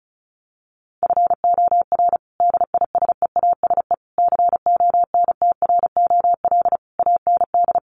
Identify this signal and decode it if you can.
Morse code, heard as a keyed tone, about 34 wpm, 700 Hz. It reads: FOR DISEASE CONTROL AND
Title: Sets of 5 Words Encore2 35wpm